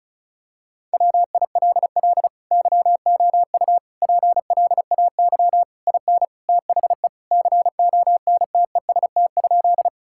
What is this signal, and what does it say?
Morse code, heard as a keyed tone, about 36 wpm, 700 Hz. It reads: WILL YOU PLAY IN THE CONTEST?
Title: Sentences from Top 1600 Words 35wpm